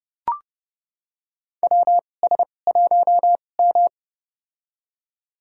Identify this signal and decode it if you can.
Morse code, heard as a keyed tone, about 30 wpm, spaced as for 22 wpm, 700 Hz. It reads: WS1M
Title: USA Call Signs - 4 Characters - 2x Character Spacing - 30wpm